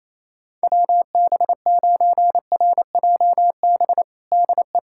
Morse code, heard as a keyed tone, about 28 wpm, 700 Hz. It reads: WB9RJB DE
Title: USA Callsign DE Callsign 28wpm